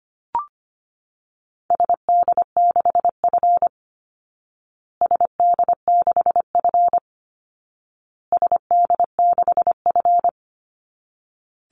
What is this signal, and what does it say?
Morse code, heard as a keyed tone, about 25 wpm, 700 Hz. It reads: SD6F SD6F SD6F
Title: Call Signs Repeated 2x - 25wpm